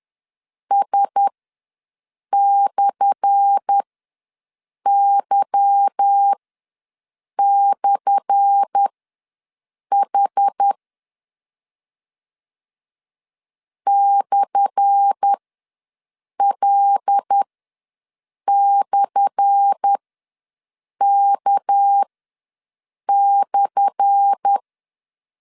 Morse code, heard as a keyed tone, about 10 wpm, spaced as for 6 wpm, 800 Hz. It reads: S/Y/H /L/K/
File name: LEZIONI / 10 SIGNS / 10A SLASH ws 10wpm 4121